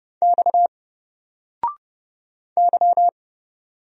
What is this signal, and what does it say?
Morse code, heard as a keyed tone, about 29 wpm, 700 Hz. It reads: X Y
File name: Single-Letter-Number-30wpm